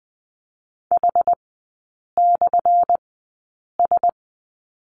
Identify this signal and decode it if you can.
Morse code, heard as a keyed tone, about 20 wpm, spaced as for 8 wpm, 700 Hz. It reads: H/S
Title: 178 Lesson 29e - Slash - Three Characters 20wpm